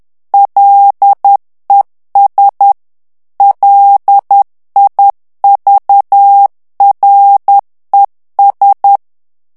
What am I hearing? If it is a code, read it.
Morse code, heard as a keyed tone, about 10 wpm, 800 Hz. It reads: LES LIVRES
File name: FRANCAIS TEXTE CLAIR 2 VITESSE 10 MOTS MINUTES